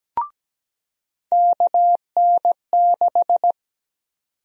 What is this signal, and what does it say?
Morse code, heard as a keyed tone, about 17 wpm, 700 Hz. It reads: KN6
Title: USA Call Sign Prefixes - 17wpm